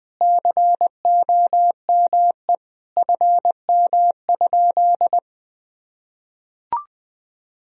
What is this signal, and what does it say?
Morse code, heard as a keyed tone, about 20 wpm, 700 Hz. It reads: COME FM?
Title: Sentences from Top 200 Words and Abbreviations 20wpm